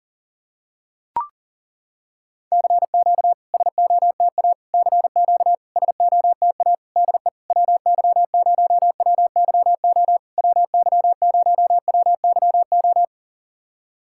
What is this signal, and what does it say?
Morse code, heard as a keyed tone, about 40 wpm, 690 Hz. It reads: CQ SOTA CQ SOTA DE WY0WYO WY0WYO
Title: SOTA QSO Practice 40wpm